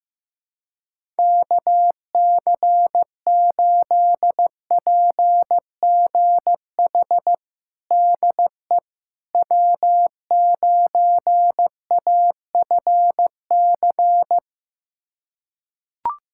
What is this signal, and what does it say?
Morse code, heard as a keyed tone, about 16 wpm, 700 Hz. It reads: KC8PGH DE W9AFC
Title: USA Callsign DE Callsign 15wpm